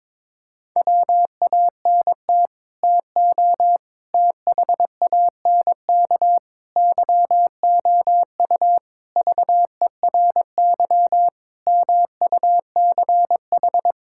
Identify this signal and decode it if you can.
Morse code, heard as a keyed tone, about 22 wpm, 700 Hz. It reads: WANT TO THANK YOU VERY MUCH
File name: Sets of 9 Words - 22wpm